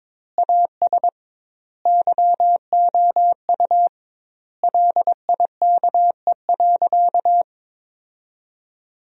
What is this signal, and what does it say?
Morse code, heard as a keyed tone, about 22 wpm, 700 Hz. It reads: AS YOU LIKE.
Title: 2x Word-Spacing - Sentences from Top 500 Words - 22wpm